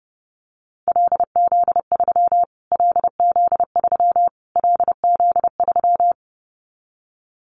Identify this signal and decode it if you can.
Morse code, heard as a keyed tone, about 28 wpm, 700 Hz. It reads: LZ3 LZ3 LZ3
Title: Callsign Prefixes for Top 25 DXCC - Repeated 3x - 30wpm